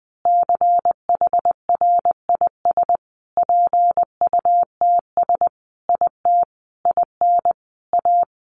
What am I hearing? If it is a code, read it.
Morse code, heard as a keyed tone, about 20 wpm, 700 Hz. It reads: CHRIS PUTS IT IN A
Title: Additional Easy Phrases 20wpm